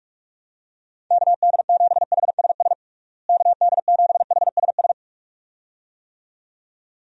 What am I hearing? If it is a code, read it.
Morse code, heard as a keyed tone, about 47 wpm, 700 Hz. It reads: KD7HSS KD7HSS
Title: USA Call Signs - 6 Characters - Repeated 2x - 45wpm